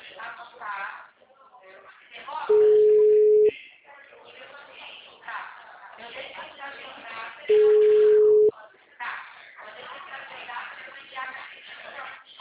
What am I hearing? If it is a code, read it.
Ringback tone (Europe, 425 Hz), with speech.